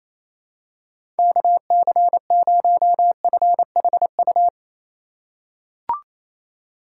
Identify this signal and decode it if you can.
Morse code, heard as a keyed tone, about 28 wpm, 700 Hz. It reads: KC0FHU
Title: USA Call Signs - 6 Characters - 2x Character Spacing - 28wpm